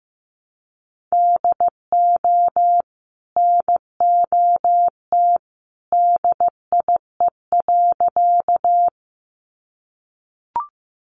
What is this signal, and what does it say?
Morse code, heard as a keyed tone, about 15 wpm, 700 Hz. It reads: DO NOT DIE.
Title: Sentences from Top 1300 Words 15wpm